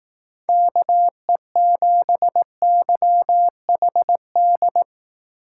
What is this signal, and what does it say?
Morse code, heard as a keyed tone, about 18 wpm, 700 Hz. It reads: KE7YHD
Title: Speed Racing - USA 2x3 Call Signs 15wpm